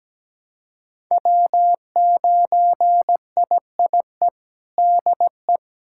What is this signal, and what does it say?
Morse code, heard as a keyed tone, about 17 wpm, 700 Hz. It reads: W9IIE DE
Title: USA Callsign DE Callsign - 5 Characters - 17wpm